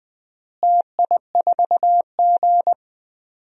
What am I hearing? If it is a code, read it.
Morse code, heard as a keyed tone, about 20 wpm, 700 Hz. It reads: TI4G
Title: DXCC-100 Call Signs - 4 Characters - Repeated 3x - 20wpm